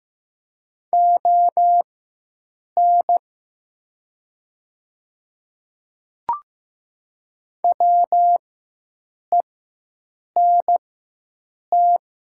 Morse code, heard as a keyed tone, about 15 wpm, spaced as for 7 wpm, 700 Hz. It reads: ON WENT
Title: 4x Character Spacing - Top 200 Words - 15wpm